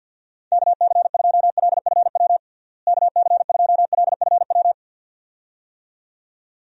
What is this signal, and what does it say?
Morse code, heard as a keyed tone, about 50 wpm, 700 Hz. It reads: KC2LFW KC2LFW
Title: USA Call Signs - 6 Characters - Repeated 2x - 50wpm